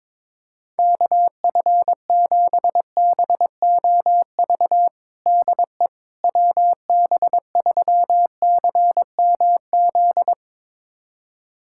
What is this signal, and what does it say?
Morse code, heard as a keyed tone, about 22 wpm, 700 Hz. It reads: KF7BOV DE WB3CMZ
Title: USA Callsign DE Callsign 22wpm